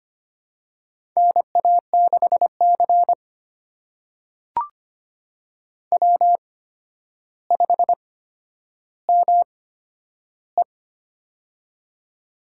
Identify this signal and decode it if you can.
Morse code, heard as a keyed tone, about 25 wpm, spaced as for 11 wpm, 700 Hz. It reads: NA6C W5ME
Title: USA Call Signs - 4 Characters - 8x Character Spacing - 25wpm